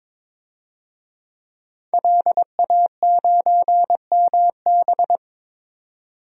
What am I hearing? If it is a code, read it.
Morse code, heard as a keyed tone, about 22 wpm, 700 Hz. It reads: LA9MB
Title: DXCC-100 Call Signs - 5 Characters - 22wpm